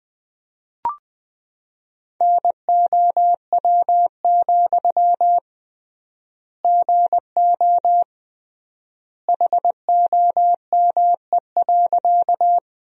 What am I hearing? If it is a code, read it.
Morse code, heard as a keyed tone, about 21 wpm, 700 Hz. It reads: NOW, GO HOME.
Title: 3x Word-Spacing - Sentences from Top 200 Words - 20wpm